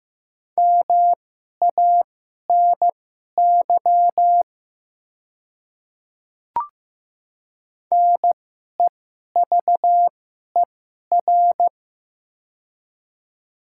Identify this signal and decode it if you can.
Morse code, heard as a keyed tone, about 15 wpm, spaced as for 11 wpm, 700 Hz. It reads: MANY NEVER
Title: 2x Character Spacing - Top 300 Words - 15wpm